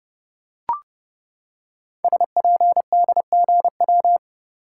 Morse code, heard as a keyed tone, about 30 wpm, 700 Hz. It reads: SPDGW
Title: Code Groups - 5 Letters - 30wpm